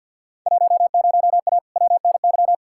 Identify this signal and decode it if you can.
Morse code, heard as a keyed tone, about 48 wpm, 700 Hz. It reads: 10A WNY
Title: Field Day Exchanges - 50wpm